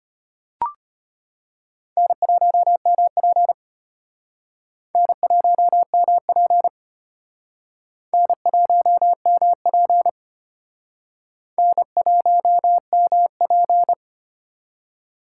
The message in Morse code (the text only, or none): N1MP N1MP N1MP N1MP